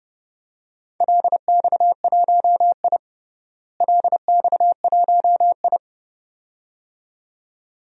Morse code, heard as a keyed tone, about 30 wpm, 700 Hz. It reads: LX1I LX1I
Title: DXCC-100 Call Signs - 4 Characters - Repeated 2x - 30wpm